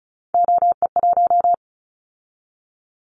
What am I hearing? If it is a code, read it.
Morse code, heard as a keyed tone, about 36 wpm, 700 Hz. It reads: OE1
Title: Callsign Prefixes for Top 25 DXCC - 35wpm